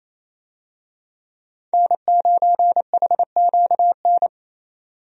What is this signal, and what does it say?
Morse code, heard as a keyed tone, about 28 wpm, 700 Hz. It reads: N9HQN
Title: USA Call Signs - 5 Characters - 28wpm